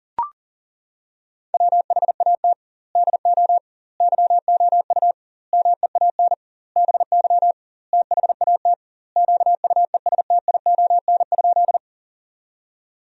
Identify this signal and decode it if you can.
Morse code, heard as a keyed tone, about 40 wpm, 700 Hz. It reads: WHAT DO YOU MEAN BY THAT QUESTION?
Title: 2x Word-Spacing - Sentences from Top 400 Words - 40wpm